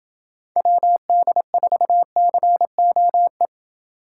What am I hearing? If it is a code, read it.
Morse code, heard as a keyed tone, about 27 wpm, 700 Hz. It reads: WD4COE